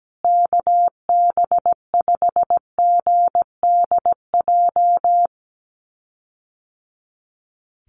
Morse code, heard as a keyed tone, about 17 wpm, 700 Hz. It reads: KB5GDJ
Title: USA Call Signs Repeated 2x 17wpm